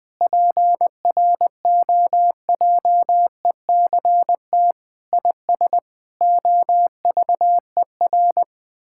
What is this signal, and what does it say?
Morse code, heard as a keyed tone, about 20 wpm, 700 Hz. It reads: PROJECT IS OVER
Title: Sentences with 8-Word Phrases - Spoken Beginning - 20wpm